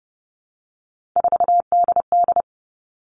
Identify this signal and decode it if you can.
Morse code, heard as a keyed tone, about 30 wpm, 700 Hz. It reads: 4DD
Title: All Cut Numbers - 3-Digit 30wpm